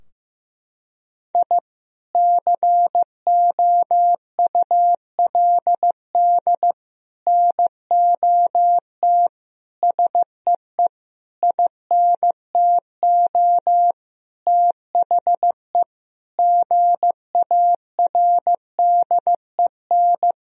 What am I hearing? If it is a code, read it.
Morse code, heard as a keyed tone, about 14 wpm, 700 Hz. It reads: I COULD NOT SEE INTO THE GARDEN
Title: Sentences from Top 1000 Words - Review Part 4 - 15wpm